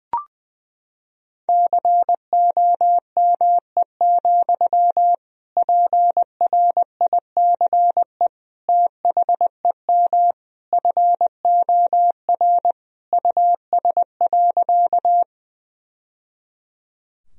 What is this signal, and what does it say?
Morse code, heard as a keyed tone, about 21 wpm, 700 Hz. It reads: COME, PRICE THEM FOR US.
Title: Sentences from Top 1000 Words - Review Part 4 - 20wpm